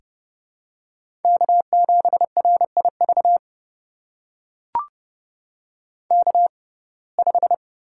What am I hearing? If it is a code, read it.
Morse code, heard as a keyed tone, about 31 wpm, spaced as for 22 wpm, 700 Hz. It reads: K7RIV K5
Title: USA Call Signs - 5 Characters - 6x Character Spacing - 30wpm